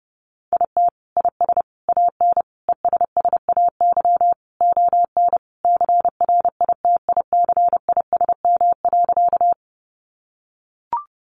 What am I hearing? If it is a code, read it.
Morse code, heard as a keyed tone, about 30 wpm, 700 Hz. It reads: IT IS AN ESSAY ON CRITICISM.
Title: Sentences from Top 1500 Words 30wpm